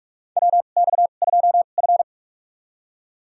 Morse code, heard as a keyed tone, about 46 wpm, spaced as for 33 wpm, 700 Hz. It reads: WX2F
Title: USA Call Signs - 2x Character Spacing - 45wpm